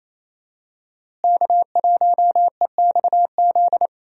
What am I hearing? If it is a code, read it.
Morse code, heard as a keyed tone, about 28 wpm, 700 Hz. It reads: K1EXZ
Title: USA Call Signs - 5 Characters - Repeated 3x - 28wpm